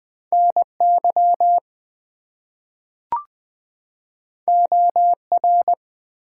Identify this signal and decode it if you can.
Morse code, heard as a keyed tone, about 20 wpm, 700 Hz. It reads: NY OR
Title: US States and Canadian Provinces 20wpm